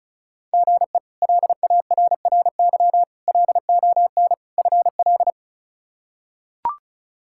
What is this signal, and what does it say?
Morse code, heard as a keyed tone, about 34 wpm, 700 Hz. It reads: GE LARRY LON FL
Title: K1USN SST QSO Practice - 35wpm